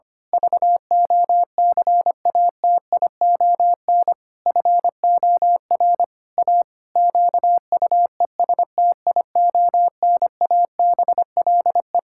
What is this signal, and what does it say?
Morse code, heard as a keyed tone, about 25 wpm, 700 Hz. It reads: VOCATION FOR A QUESTIONABLE
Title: Pangram Challenge v2 - 25wpm